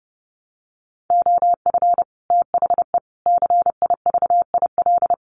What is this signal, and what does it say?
Morse code, heard as a keyed tone, about 30 wpm, 700 Hz. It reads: OF THE CIVIL
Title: Sets of 5 Words Encore2 30wpm